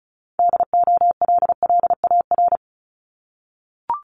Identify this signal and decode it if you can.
Morse code, heard as a keyed tone, about 36 wpm, 700 Hz. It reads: DOLLAR
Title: Multi-letter Phonemes 35wpm